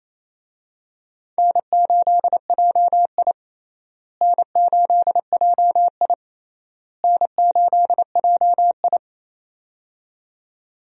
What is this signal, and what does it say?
Morse code, heard as a keyed tone, about 28 wpm, 700 Hz. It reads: N8JI N8JI N8JI
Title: USA Call Signs - 4 Characters - Repeated 3x - 28wpm